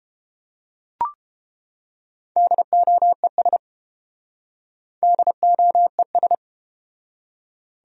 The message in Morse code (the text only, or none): DOES DOES